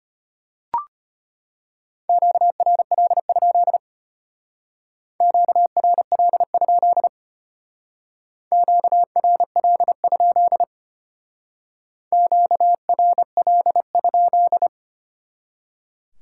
Morse code, heard as a keyed tone, about 32 wpm, 700 Hz. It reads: QRL? QRL? QRL? QRL?